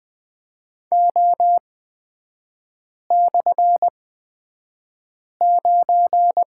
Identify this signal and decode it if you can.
Morse code, heard as a keyed tone, about 20 wpm, spaced as for 5 wpm, 700 Hz. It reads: O/9
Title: 224 Lesson 36e - BT - Three Characters 20wpm